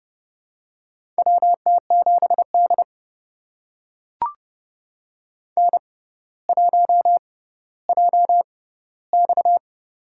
Morse code, heard as a keyed tone, about 30 wpm, spaced as for 15 wpm, 700 Hz. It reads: WT7D N1JX
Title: USA Call Signs - 4 Characters - 6x Character Spacing - 30wpm